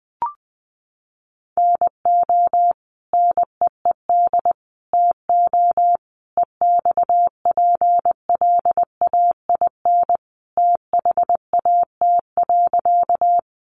Morse code, heard as a keyed tone, about 20 wpm, 700 Hz. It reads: NO NEED TO EXPLAIN THAT.
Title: Sentences from Top 1100 Words 20wpm